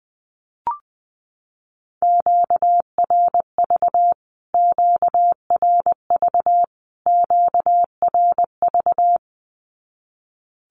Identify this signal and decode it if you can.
Morse code, heard as a keyed tone, about 20 wpm, 700 Hz. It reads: QRV QRV QRV
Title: Q-Codes Repeated 3x 20wpm